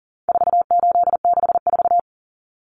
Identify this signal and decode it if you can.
Morse code, heard as a keyed tone, about 40 wpm, 700 Hz. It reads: 4864
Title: Numbers - 4-Digit 40wpm